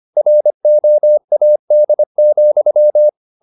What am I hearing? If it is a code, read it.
Morse code, heard as a keyed tone, about 25 wpm, 600 Hz. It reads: ROAD,
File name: the old man and the sea part 1 25wpm